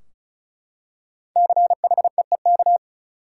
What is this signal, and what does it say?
Morse code, heard as a keyed tone, about 34 wpm, 700 Hz. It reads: CHEEK
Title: Five Letter Words - 35wpm